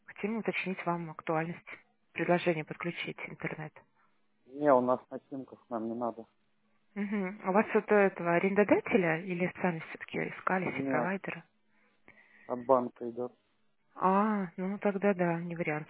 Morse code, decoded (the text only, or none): none